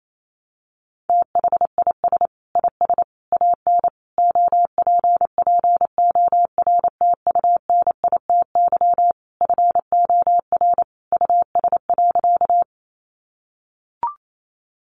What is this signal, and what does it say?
Morse code, heard as a keyed tone, about 28 wpm, 700 Hz. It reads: THIS IS AN OPPORTUNITY FOR US.
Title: Sentences from Top 1000 Words 28wpm